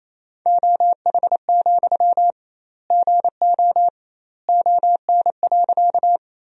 Morse code, heard as a keyed tone, about 28 wpm, 700 Hz. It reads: OH, GO ON.
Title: 2x Word-Spacing - Sentences from Top 500 Words - 28wpm